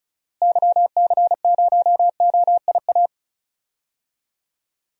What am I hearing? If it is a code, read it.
Morse code, heard as a keyed tone, about 36 wpm, 700 Hz. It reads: YC0OIA
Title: DXCC-100 Call Signs - 6 Characters - Repeated 2x - 35wpm